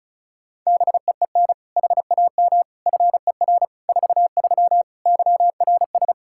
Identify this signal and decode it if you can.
Morse code, heard as a keyed tone, about 34 wpm, 700 Hz. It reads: BEEN HAM FER 43 YRS
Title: Practice QSO - Exchange 3 - More Info - 35wpm